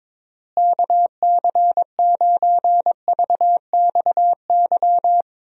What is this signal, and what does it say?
Morse code, heard as a keyed tone, about 21 wpm, 700 Hz. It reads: KC9VXY
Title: USA Call Signs - 6 Characters - Repeated 2x - 22wpm